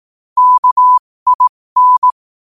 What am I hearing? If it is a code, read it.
Morse code, heard as a keyed tone, about 18 wpm, spaced as for 15 wpm, 1000 Hz. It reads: KIN
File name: qotdfiles / 15wpm